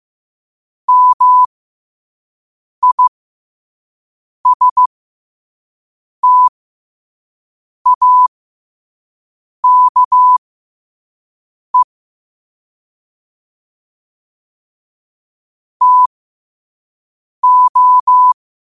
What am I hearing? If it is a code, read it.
Morse code, heard as a keyed tone, about 15 wpm, spaced as for 5 wpm, 1000 Hz. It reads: MISTAKE TO
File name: qotdfiles / 5wpm